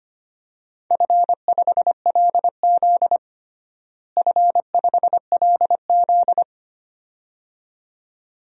Morse code, heard as a keyed tone, about 25 wpm, 700 Hz. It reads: F5LZ F5LZ
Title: DXCC-100 Call Signs - 4 Characters - Repeated 2x - 25wpm